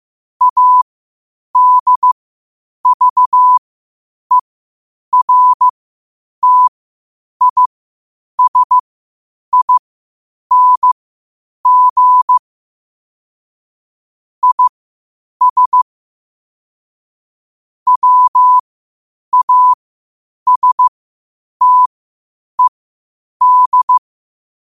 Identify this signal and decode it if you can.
Morse code, heard as a keyed tone, about 15 wpm, spaced as for 8 wpm, 1000 Hz. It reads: ADVERTISING IS WASTED
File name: qotdfiles / 8wpm